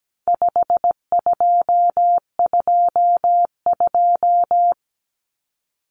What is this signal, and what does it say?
Morse code, heard as a keyed tone, about 17 wpm, 700 Hz. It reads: 5222
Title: Numbers - 4-Digit 17wpm